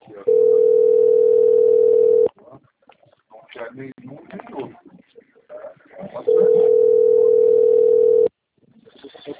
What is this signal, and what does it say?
Ringback tone (North America), 440 + 480 Hz, with speech.
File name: out- - - - - .48237